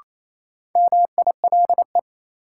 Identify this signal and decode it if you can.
Morse code, heard as a keyed tone, about 28 wpm, 700 Hz. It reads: MILE
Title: Sentences from Top 400 Words - New Word Drill - 28wpm